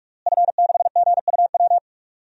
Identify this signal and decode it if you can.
Morse code, heard as a keyed tone, about 46 wpm, 700 Hz. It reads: F6GUW
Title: DXCC-100 Call Signs - 5 Characters - Repeated 2x - 45wpm